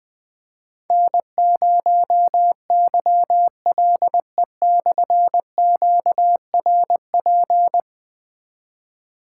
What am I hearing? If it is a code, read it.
Morse code, heard as a keyed tone, about 20 wpm, 700 Hz. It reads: N0YLE/QRP
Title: Difficult Call Signs 20wpm